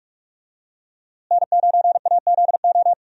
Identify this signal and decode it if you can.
Morse code, heard as a keyed tone, about 41 wpm, 700 Hz. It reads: N9AZO
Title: USA Call Signs - 5 Characters - Repeated 2x - 45wpm